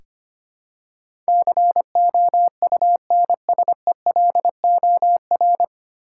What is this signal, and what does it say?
Morse code, heard as a keyed tone, about 25 wpm, 700 Hz. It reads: COUNSELOR
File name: practice / 25wpm / 9 Letter Words - First 3 Letters Given - 25wpm